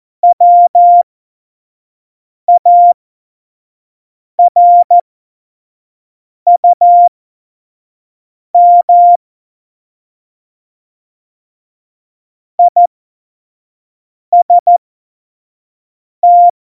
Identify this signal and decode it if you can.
Morse code, heard as a keyed tone, about 14 wpm, spaced as for 5 wpm, 700 Hz. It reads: WARUM IST
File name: Morsekurs / Witze / Witz 08. August 17, Tempo 25 F70